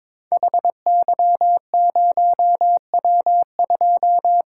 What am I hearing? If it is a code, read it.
Morse code, heard as a keyed tone, about 23 wpm, 700 Hz. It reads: HY0W2
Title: Code Groups - 5 Letter:Numbers - 22wpm